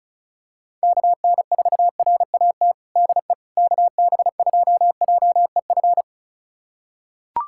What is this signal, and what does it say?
Morse code, heard as a keyed tone, about 34 wpm, 700 Hz. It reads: KN4RAT DE KB2JEF
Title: USA Callsign DE Callsign - 6 Characters - 35wpm